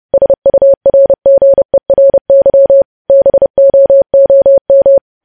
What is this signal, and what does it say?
Morse code, heard as a keyed tone, about 30 wpm, 550 Hz. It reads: SURGERY BOOM
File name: podcast / News headlines-1x-30x30